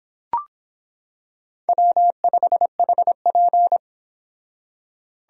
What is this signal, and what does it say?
Morse code, heard as a keyed tone, about 25 wpm, 700 Hz. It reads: W5HP